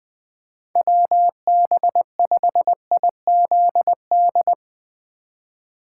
Morse code, heard as a keyed tone, about 20 wpm, 700 Hz. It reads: WB5IZD
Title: USA Call Signs - 4x Character Spacing - 20wpm